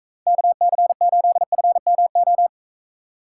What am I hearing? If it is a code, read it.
Morse code, heard as a keyed tone, about 43 wpm, 700 Hz. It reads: KC8FMO